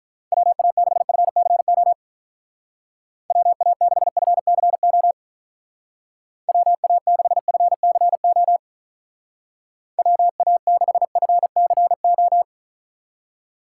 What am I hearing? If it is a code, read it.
Morse code, heard as a keyed tone, about 44 wpm, 700 Hz. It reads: WA6FCO WA6FCO WA6FCO WA6FCO